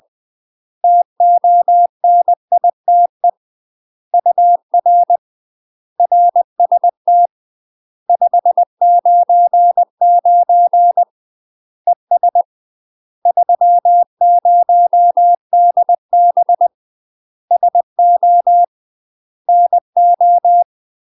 Morse code, heard as a keyed tone, about 20 wpm, 700 Hz. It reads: TONITE UR RST 599 ES 30DB SO NO
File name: 20WPM